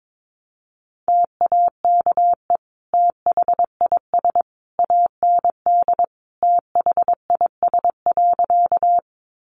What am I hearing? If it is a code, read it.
Morse code, heard as a keyed tone, about 22 wpm, 700 Hz. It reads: TAKE THIS AND THIS.
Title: Sentences from Top 200 Words 22wpm